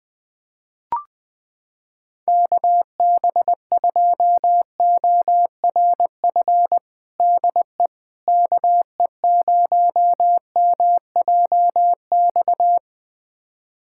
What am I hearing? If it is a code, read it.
Morse code, heard as a keyed tone, about 20 wpm, 700 Hz. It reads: KB2ORF DE KE0MJX
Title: USA Callsign DE Callsign - 6 Characters - 20wpm